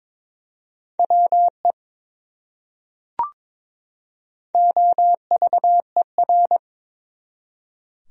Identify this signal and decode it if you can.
Morse code, heard as a keyed tone, about 22 wpm, 700 Hz. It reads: WE OVER
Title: Top 100 Words 22wpm